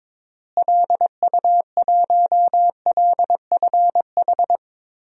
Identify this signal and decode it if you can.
Morse code, heard as a keyed tone, about 22 wpm, 700 Hz. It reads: LU1LFH
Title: DXCC-100 Call Signs - 6 Characters - Repeated 3x - 22wpm